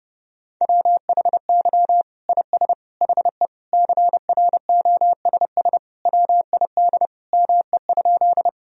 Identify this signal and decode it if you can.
Morse code, heard as a keyed tone, about 30 wpm, 700 Hz. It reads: WHY IS HE CROSS WID ME?
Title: Sentences from Top 300 Words and Abbreviations 30wpm